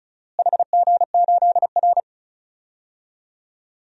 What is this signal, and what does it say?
Morse code, heard as a keyed tone, about 36 wpm, 700 Hz. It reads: HG8R
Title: DXCC-100 Call Signs - 4 Characters - Repeated 2x - 35wpm